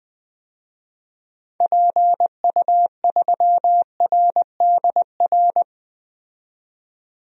Morse code, heard as a keyed tone, about 20 wpm, 700 Hz. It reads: PU3RDR
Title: DXCC-100 Call Signs - 6 Characters - 20wpm